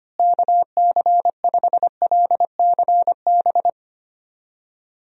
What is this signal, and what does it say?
Morse code, heard as a keyed tone, about 25 wpm, 700 Hz. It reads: KC5LCB
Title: USA Call Signs - 6 Characters - Repeated 2x - 25wpm